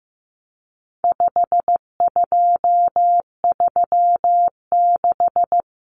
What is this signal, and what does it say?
Morse code, heard as a keyed tone, about 15 wpm, 700 Hz. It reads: 5236
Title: Numbers - 4-Digit 15wpm